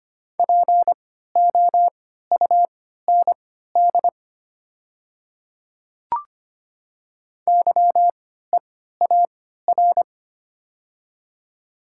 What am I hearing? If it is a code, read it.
Morse code, heard as a keyed tone, about 25 wpm, spaced as for 14 wpm, 690 Hz. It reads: POUND YEAR
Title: 3x Character Spacing - Top 500 Words - 25wpm